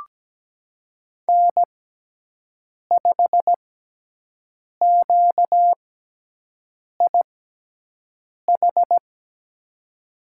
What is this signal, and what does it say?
Morse code, heard as a keyed tone, about 17 wpm, spaced as for 6 wpm, 700 Hz. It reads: N5QIH
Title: USA Call Signs - 5 Characters - 6x Character Spacing - 17wpm